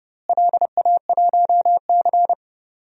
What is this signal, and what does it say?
Morse code, heard as a keyed tone, about 30 wpm, 700 Hz. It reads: LA1C
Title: DXCC-100 Call Signs - 4 Characters - Repeated 3x - 30wpm